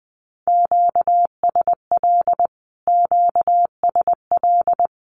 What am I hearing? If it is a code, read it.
Morse code, heard as a keyed tone, about 20 wpm, 700 Hz. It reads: QSL QSL
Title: Q-Codes Repeated 3x 20wpm